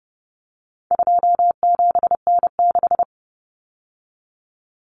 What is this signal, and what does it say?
Morse code, heard as a keyed tone, about 30 wpm, 700 Hz. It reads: 27N6
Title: Common Cut Numbers - 4-Digit 30wpm